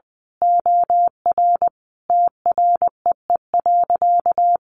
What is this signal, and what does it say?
Morse code, heard as a keyed tone, about 20 wpm, 700 Hz. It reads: OR TREE.
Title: A Sound Of Thunder 20wpm